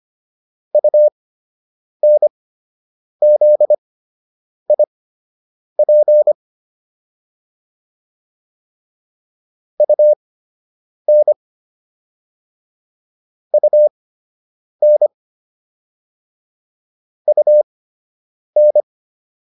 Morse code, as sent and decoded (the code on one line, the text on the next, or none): ..- -. --.. .. .--. / ..- -. / ..- -. / ..- -.
UNZIP UN UN UN